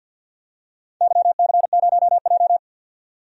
Morse code, heard as a keyed tone, about 52 wpm, 700 Hz. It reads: YC0J
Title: DXCC-100 Call Signs - 4 Characters - 50wpm